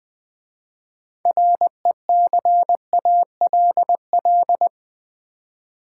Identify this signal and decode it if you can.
Morse code, heard as a keyed tone, about 20 wpm, 700 Hz. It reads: RECALL
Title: 6 Letter Words - First 3 Letters Given - 20wpm